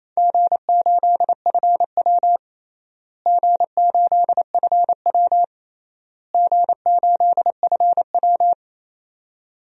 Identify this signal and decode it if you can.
Morse code, heard as a keyed tone, about 28 wpm, 700 Hz. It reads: G8FW G8FW G8FW
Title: DXCC-100 Call Signs - 4 Characters - Repeated 3x - 28wpm